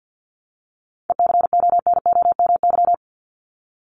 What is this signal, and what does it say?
Morse code, heard as a keyed tone, about 50 wpm, 700 Hz. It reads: ECONOMY
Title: Top 501-750 Words 50wpm